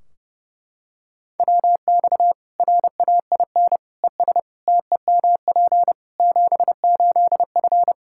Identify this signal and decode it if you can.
Morse code, heard as a keyed tone, about 32 wpm, 700 Hz. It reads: WX RAIN ES TEMP 78F
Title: K1USN SST QSO Practice 30wpm